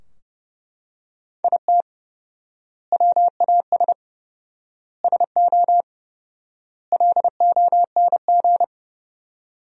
Morse code, heard as a keyed tone, about 30 wpm, 700 Hz. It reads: IT WAS SO LONG
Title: 4x Word-Spacing - Sentences from Top 500 Words - 30wpm